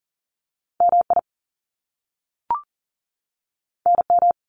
Morse code, heard as a keyed tone, about 39 wpm, 700 Hz. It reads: MI NM
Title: US States 40wpm